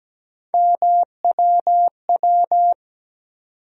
Morse code, heard as a keyed tone, about 17 wpm, 700 Hz. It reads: MWW
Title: USA Call Sign Postfixes - 3 Letters 17wpm